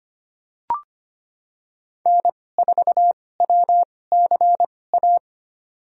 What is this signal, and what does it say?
Morse code, heard as a keyed tone, about 25 wpm, spaced as for 18 wpm, 700 Hz. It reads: N4WCA
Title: USA Call Signs - 5 Characters - 2x Character Spacing - 25wpm